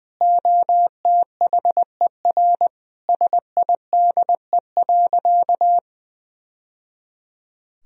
Morse code, heard as a keyed tone, about 20 wpm, 700 Hz. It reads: OTHER SIDE.
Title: Sentences from Top 1000 Words - Review Part 1 - 20wpm